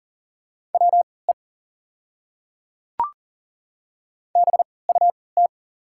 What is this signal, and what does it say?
Morse code, heard as a keyed tone, about 40 wpm, spaced as for 23 wpm, 700 Hz. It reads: WE BUT
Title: 3x Character Spacing - Top 100 Words - 40wpm